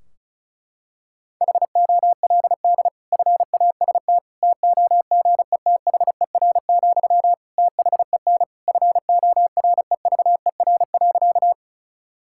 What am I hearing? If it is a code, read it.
Morse code, heard as a keyed tone, about 34 wpm, 700 Hz. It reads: HOLD FAST TOGETHER, THEN FOREVER.
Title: Sentences from Top 1600 Words 35wpm